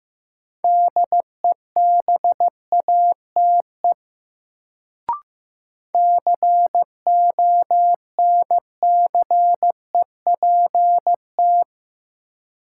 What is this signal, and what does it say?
Morse code, heard as a keyed tone, about 15 wpm, 700 Hz. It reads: DEBATE CONCEPT
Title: Sentences from Top 1100 Words - New Word Drill - 15wpm